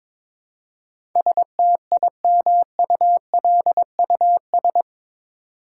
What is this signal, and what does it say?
Morse code, heard as a keyed tone, about 22 wpm, 700 Hz. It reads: STIMULUS
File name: practice / 22wpm / Eight Letter Words - 22wpm